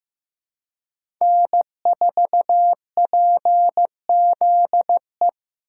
Morse code, heard as a keyed tone, about 15 wpm, 700 Hz. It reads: N4PZE
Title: USA Call Signs - 2x Character Spacing - 15wpm